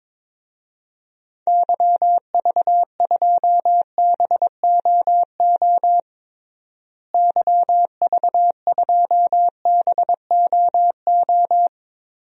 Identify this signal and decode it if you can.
Morse code, heard as a keyed tone, about 22 wpm, 700 Hz. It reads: YV2BOO YV2BOO
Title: DXCC-100 Call Signs - 6 Characters - Repeated 2x - 22wpm